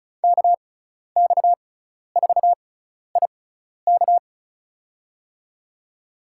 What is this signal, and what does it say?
Morse code, heard as a keyed tone, about 34 wpm, spaced as for 12 wpm, 700 Hz. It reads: KX4IK
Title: USA Call Signs - 5 Characters - 6x Character Spacing - 35wpm